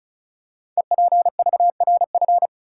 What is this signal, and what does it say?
Morse code, heard as a keyed tone, about 36 wpm, 700 Hz. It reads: EPVRF
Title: Code Groups - 5 Letters - 35wpm